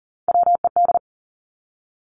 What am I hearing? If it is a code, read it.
Morse code, heard as a keyed tone, about 38 wpm, 700 Hz. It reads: WED
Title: Three Challenging Letter-Number Combinations 40wpm